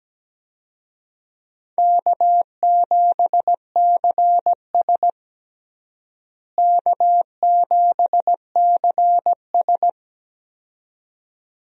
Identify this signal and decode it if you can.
Morse code, heard as a keyed tone, about 17 wpm, 700 Hz. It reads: K7CS K7CS
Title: USA Call Signs - 4 Characters - Repeated 2x - 17wpm